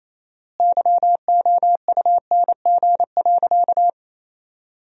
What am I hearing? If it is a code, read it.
Morse code, heard as a keyed tone, about 28 wpm, 700 Hz. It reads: YOUNG.
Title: 4x Word-Spacing - Sentences from Top 400 Words - 28wpm